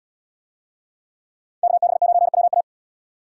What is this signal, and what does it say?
Morse code, heard as a keyed tone, about 72 wpm, 700 Hz. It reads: KB9WA